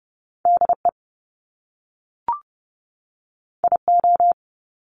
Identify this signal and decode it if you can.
Morse code, heard as a keyed tone, about 30 wpm, 700 Hz. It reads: DE IO
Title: Top Two Letter Combinations in Words 30wpm